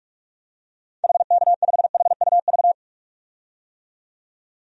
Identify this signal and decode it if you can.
Morse code, heard as a keyed tone, about 46 wpm, 700 Hz. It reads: HK5HUV
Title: DXCC-100 Call Signs - 6 Characters - 45wpm